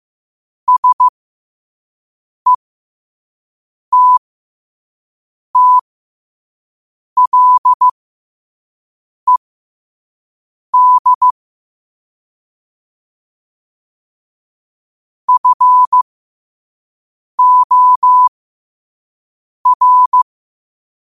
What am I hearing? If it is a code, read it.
Morse code, heard as a keyed tone, about 15 wpm, spaced as for 5 wpm, 1000 Hz. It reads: SETTLED FOR
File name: qotdfiles / 5wpm